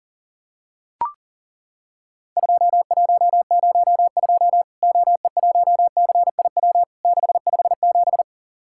Morse code, heard as a keyed tone, about 40 wpm, 700 Hz. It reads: 2102 OE1CIW 657
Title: Worked All Europe DX Contest - QTC Exchange - 40wpm